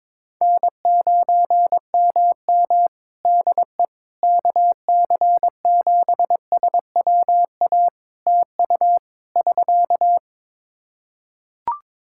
Morse code, heard as a keyed tone, about 22 wpm, 700 Hz. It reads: N9MM DE KC7SWA TU <SK>
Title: Practice QSO - Exchange 4 - Ending - 22wpm